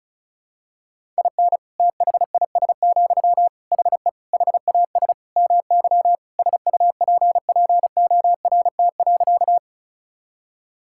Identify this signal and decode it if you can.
Morse code, heard as a keyed tone, about 36 wpm, 700 Hz. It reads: IN THIS, HE HAS MY SUPPORT.
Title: Sentences from Top 1000 Words - Review Part 5 - 35wpm